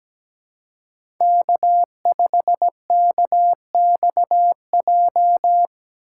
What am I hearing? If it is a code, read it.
Morse code, heard as a keyed tone, about 17 wpm, 700 Hz. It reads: K5KXJ
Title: K1USN SST QSO Practice - 17wpm